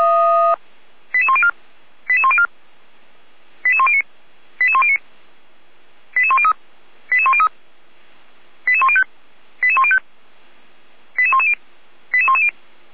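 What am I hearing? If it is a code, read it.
Five-tone selective call (ZVEI), heard as quick runs of five tones, tones of about 70 ms. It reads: 88104 88104 88109 88109 88103 88103 88105 88105 88110 88110
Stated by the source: ZVEI five-tone selective call received by SDR.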